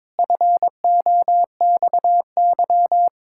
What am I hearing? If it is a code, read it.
Morse code, heard as a keyed tone, about 22 wpm, 700 Hz. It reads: FOXY
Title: Pangram Challenge v2 - 22wpm